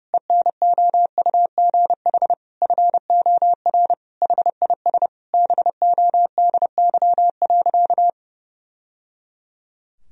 Morse code, heard as a keyed tone, about 30 wpm, 700 Hz. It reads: ENOUGH FOR HIS BODY.
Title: Sentences from Top 1000 Words - Review Part 2 - 30wpm